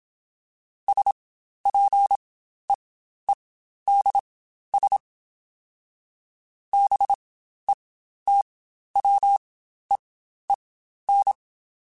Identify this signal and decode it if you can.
Morse code, heard as a keyed tone, about 27 wpm, spaced as for 12 wpm, 800 Hz. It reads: SPEEDS BETWEEN
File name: Spaced Characters / Spaced Characters-mp3s / Weather4 24 sp @25